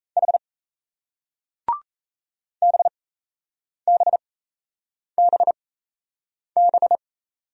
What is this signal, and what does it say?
Morse code, heard as a keyed tone, about 38 wpm, spaced as for 8 wpm, 700 Hz. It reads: H BBBB